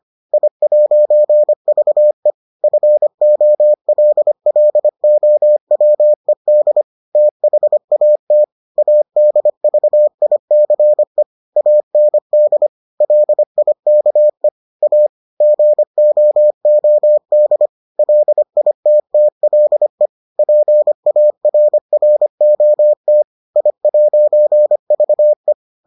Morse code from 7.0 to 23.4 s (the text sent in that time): THAT ADVICE AND LIKE A GOOD LITTLE PARROT